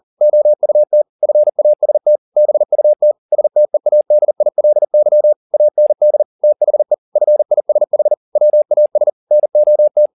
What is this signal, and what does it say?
Morse code, heard as a keyed tone, about 40 wpm, 600 Hz. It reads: OUT FAST BUT STEADILY AND THE FISH WAS NOT